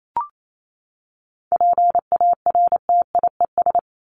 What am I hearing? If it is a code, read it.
Morse code, heard as a keyed tone, about 28 wpm, 700 Hz. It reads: PARTIES
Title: ES:S Suffix - Top 100 Words with Base Words 28wpm